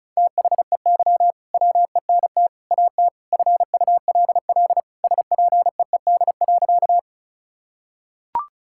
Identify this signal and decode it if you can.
Morse code, heard as a keyed tone, about 34 wpm, 700 Hz. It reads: THEY WENT AT FULL SPEED.
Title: Sentences from Top 1000 Words - Review Part 5 - 35wpm